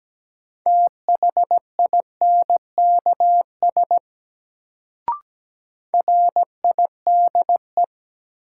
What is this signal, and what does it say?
Morse code, heard as a keyed tone, about 17 wpm, 700 Hz. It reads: THINKS RIDE
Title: Sentences from Top 1000 Words - Review Part 6 - New Word Drill - 17wpm